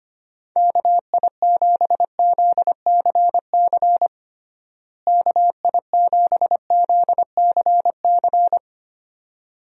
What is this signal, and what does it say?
Morse code, heard as a keyed tone, about 25 wpm, 700 Hz. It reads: KI7ZCC KI7ZCC
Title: USA Call Signs - 6 Characters - Repeated 2x - 25wpm